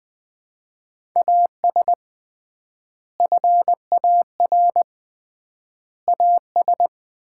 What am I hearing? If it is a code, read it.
Morse code, heard as a keyed tone, about 20 wpm, 700 Hz. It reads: AS FAR AS
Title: 3x Word-Spacing - Sets of 4 Words - 20wpm